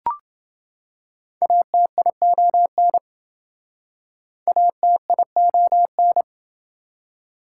Morse code, heard as a keyed tone, about 28 wpm, 700 Hz. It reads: ATION ATION